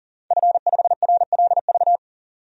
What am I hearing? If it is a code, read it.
Morse code, heard as a keyed tone, about 40 wpm, 700 Hz. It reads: F5RLV
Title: DXCC-100 Call Signs - 5 Characters - 40wpm